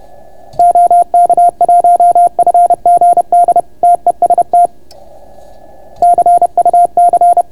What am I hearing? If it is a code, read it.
Morse code, heard as a keyed tone, about 30 wpm, 690 Hz. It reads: OK1FGD TEST CUC